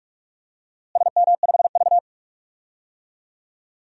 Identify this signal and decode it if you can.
Morse code, heard as a keyed tone, about 46 wpm, 700 Hz. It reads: SM5V
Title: DXCC-100 Call Signs - 4 Characters - Repeated 2x - 45wpm